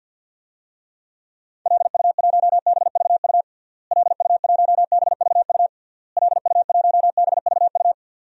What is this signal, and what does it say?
Morse code, heard as a keyed tone, about 46 wpm, 700 Hz. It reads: LU1BVU LU1BVU LU1BVU
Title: DXCC-100 Call Signs - 6 Characters - Repeated 3x - 50wpm